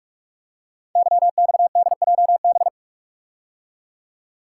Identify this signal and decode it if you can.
Morse code, heard as a keyed tone, about 44 wpm, 700 Hz. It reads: YXDJB
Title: Code Groups - 5 Letters - 45wpm